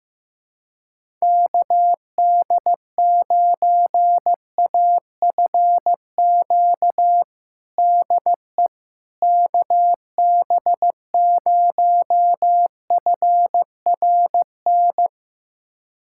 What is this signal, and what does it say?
Morse code, heard as a keyed tone, about 16 wpm, 700 Hz. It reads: KD9AFQ DE KB0FRN
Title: USA Callsign DE Callsign - 6 Characters - 15wpm